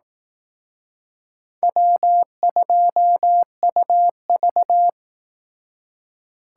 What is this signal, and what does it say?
Morse code, heard as a keyed tone, about 18 wpm, 700 Hz. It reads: W2UV
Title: Speed Racing - USA 1x2 Call Signs 15wpm